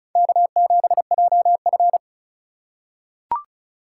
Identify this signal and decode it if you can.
Morse code, heard as a keyed tone, about 36 wpm, 700 Hz. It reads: K7JF
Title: USA Call Signs - 4 Characters - 3x Character Spacing - 35wpm